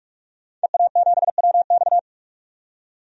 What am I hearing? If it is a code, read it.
Morse code, heard as a keyed tone, about 46 wpm, 700 Hz. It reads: EA7WX
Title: DXCC-100 Call Signs - 5 Characters - 45wpm